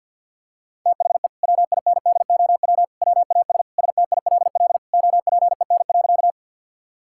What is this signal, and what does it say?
Morse code, heard as a keyed tone, about 51 wpm, 700 Hz. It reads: THE WINDOW WAS STILL OPEN.
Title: Sentences from Top 1600 Words 50wpm